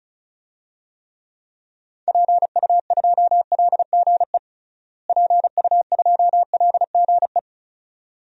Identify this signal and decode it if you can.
Morse code, heard as a keyed tone, about 34 wpm, 700 Hz. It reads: PU2LGE PU2LGE
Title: DXCC-100 Call Signs - 6 Characters - Repeated 2x - 35wpm